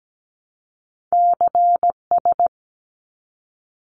Morse code, heard as a keyed tone, about 17 wpm, 700 Hz. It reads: CS
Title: Abbreviations 17wpm